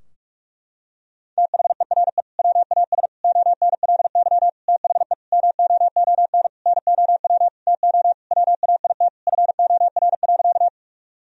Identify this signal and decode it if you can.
Morse code, heard as a keyed tone, about 45 wpm, 700 Hz. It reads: THERE WAS ONLY THE MOON NOW TO WAIT FOR.
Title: Sentences from Top 1000 Words - Review Part 3 - 45wpm